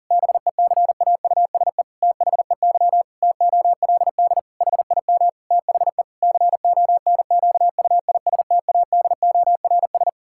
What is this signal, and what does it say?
Morse code, heard as a keyed tone, about 40 wpm, 700 Hz. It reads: BECAUSE THEY TOLD HIM THE CONQUISTADORS
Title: Pangram Challenge v2 - 40wpm